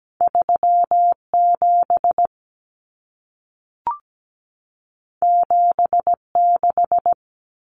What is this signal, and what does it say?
Morse code, heard as a keyed tone, about 17 wpm, 700 Hz. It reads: 37 76
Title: Common Cut Numbers - 2-Digit 17wpm